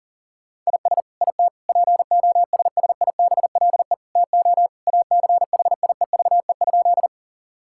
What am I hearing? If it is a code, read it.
Morse code, heard as a keyed tone, about 40 wpm, 700 Hz. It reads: IS IT POSSIBLE TO ACHIEVE?
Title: Sentences from Top 1000 Words - Review Part 6 - 40wpm